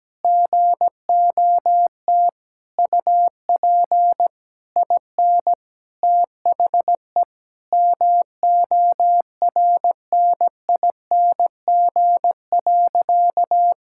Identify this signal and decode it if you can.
Morse code, heard as a keyed tone, about 17 wpm, 700 Hz. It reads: GOT UP IN THE MORNING.
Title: Sentences from Top 1000 Words - Review Part 5 - 17wpm